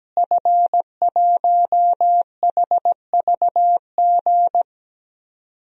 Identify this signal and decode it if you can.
Morse code, heard as a keyed tone, about 17 wpm, 700 Hz. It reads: F1HVG
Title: DXCC-100 Call Signs - 5 Characters - Repeated 2x - 17wpm